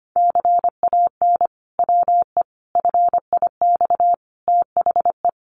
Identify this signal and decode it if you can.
Morse code, heard as a keyed tone, about 25 wpm, 700 Hz. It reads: CAN WE FIX THE
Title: Sentences from Top 1400 Words 25wpm